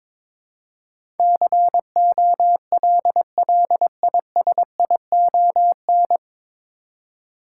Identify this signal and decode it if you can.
Morse code, heard as a keyed tone, about 22 wpm, 700 Hz. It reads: COLLISION
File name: 9 Letter Words - First 3 Letters Given - 22wpm